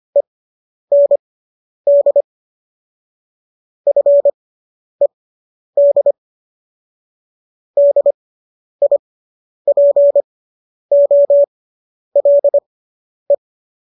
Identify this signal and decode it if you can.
Morse code, heard as a keyed tone, about 25 wpm, spaced as for 10 wpm, 570 Hz. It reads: END FED DIPOLE